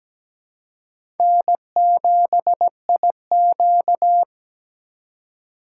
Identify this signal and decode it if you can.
Morse code, heard as a keyed tone, about 17 wpm, 700 Hz. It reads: N7IQ
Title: USA Call Signs - 3x Character Spacing - 17wpm